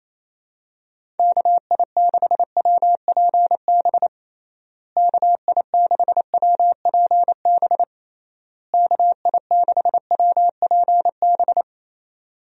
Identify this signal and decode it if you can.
Morse code, heard as a keyed tone, about 28 wpm, 700 Hz. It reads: KI6WPB KI6WPB KI6WPB
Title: USA Call Signs - 6 Characters - Repeated 3x - 28wpm